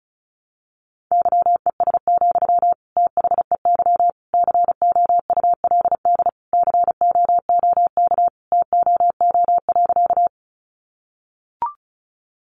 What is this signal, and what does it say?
Morse code, heard as a keyed tone, about 34 wpm, 700 Hz. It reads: YES, THEY COULD COOK TOO.
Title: Sentences from Top 700 Words 35wpm